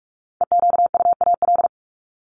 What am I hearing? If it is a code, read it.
Morse code, heard as a keyed tone, about 44 wpm, 700 Hz. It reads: EQUAL
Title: LY Suffix - Top 100 Words with Base Words 45wpm